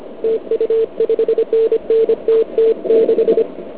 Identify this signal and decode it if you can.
Morse code, heard as a keyed tone, about 36 wpm, spaced as for 32 wpm, 440 Hz. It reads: TU 5NN TT6
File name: ARCHIV / KX3 / OK1IF CUC175